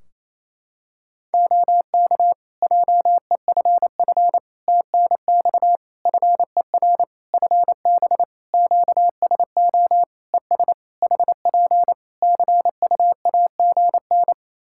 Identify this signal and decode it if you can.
Morse code, heard as a keyed tone, about 28 wpm, 700 Hz. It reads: OK JEFF TNX FER FB QSO ES HP CUAGN
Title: K1USN SST QSO Practice 28wpm